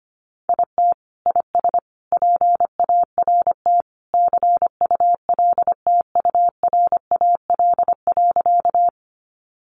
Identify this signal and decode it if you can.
Morse code, heard as a keyed tone, about 25 wpm, 700 Hz. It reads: IT IS PART CULTURAL.
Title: Sentences from Top 1500 Words 25wpm